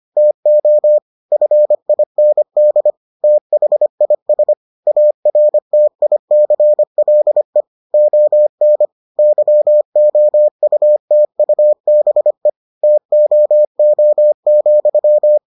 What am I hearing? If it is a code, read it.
Morse code, heard as a keyed tone, about 25 wpm, 600 Hz. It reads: TO FIND THIS ARTICLE ON YOUTUBE TOO,